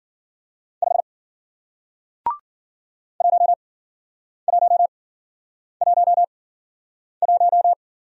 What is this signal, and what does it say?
Morse code, heard as a keyed tone, about 53 wpm, spaced as for 9 wpm, 700 Hz. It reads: 5 1111